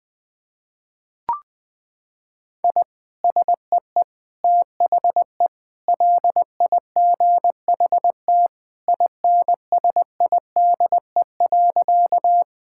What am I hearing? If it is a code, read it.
Morse code, heard as a keyed tone, about 20 wpm, 700 Hz. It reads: I SEE THE LIGHT INSIDE.
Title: Sentences from Top 1000 Words - Review Part 6 - 20wpm